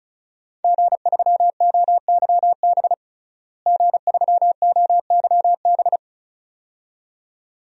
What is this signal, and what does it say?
Morse code, heard as a keyed tone, about 36 wpm, 700 Hz. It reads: G3OYB G3OYB
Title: DXCC-100 Call Signs - 5 Characters - Repeated 2x - 35wpm